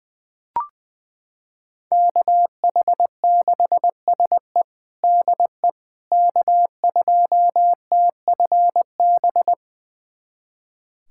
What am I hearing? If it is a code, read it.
Morse code, heard as a keyed tone, about 21 wpm, 700 Hz. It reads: KH6SE DE K2TFB
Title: USA Callsign DE Callsign - 5 Characters - 20wpm